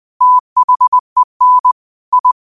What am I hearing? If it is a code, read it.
Morse code, heard as a keyed tone, about 20 wpm, 1000 Hz. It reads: THEN I